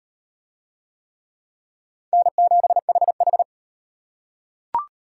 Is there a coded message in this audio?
Yes: Morse code from 2.1 to 3.4 s, heard as a keyed tone, about 32 wpm, 700 Hz. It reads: N7HH